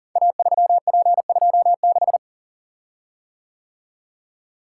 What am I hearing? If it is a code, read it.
Morse code, heard as a keyed tone, about 42 wpm, 690 Hz. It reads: A3P26
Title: Code Groups - 5 Letter:Numbers - 40wpm